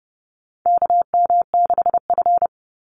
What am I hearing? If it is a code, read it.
Morse code, heard as a keyed tone, about 30 wpm, 700 Hz. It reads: KM6F
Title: USA Call Signs 30wpm